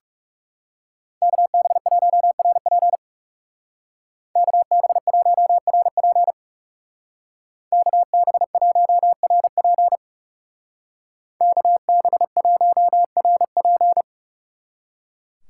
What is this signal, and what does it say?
Morse code, heard as a keyed tone, about 39 wpm, 700 Hz. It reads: KB1RP KB1RP KB1RP KB1RP